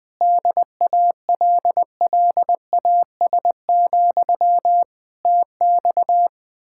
Morse code, heard as a keyed tone, about 20 wpm, 700 Hz. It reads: DALLAS, TX
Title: Speed-Racing Common QSO Phrases - 20wpm